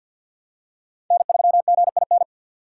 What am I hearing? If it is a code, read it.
Morse code, heard as a keyed tone, about 52 wpm, 700 Hz. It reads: N3GIN
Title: USA Call Signs - 3x Character Spacing - 50wpm